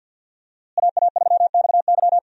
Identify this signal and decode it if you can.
Morse code, heard as a keyed tone, about 47 wpm, 700 Hz. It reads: AA3XY
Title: USA Call Signs - 5 Characters - Repeated 2x - 50wpm